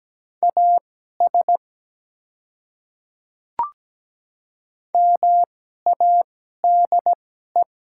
Morse code, heard as a keyed tone, about 17 wpm, spaced as for 12 wpm, 700 Hz. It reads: AS MADE
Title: 2x Character Spacing - Top 200 Words - 17wpm